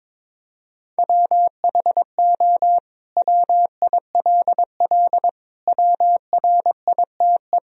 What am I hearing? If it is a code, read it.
Morse code, heard as a keyed tone, about 21 wpm, 700 Hz. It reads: WHO WILL WRITE
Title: Sentences from Top 1000 Words - Review Part 3 - 22wpm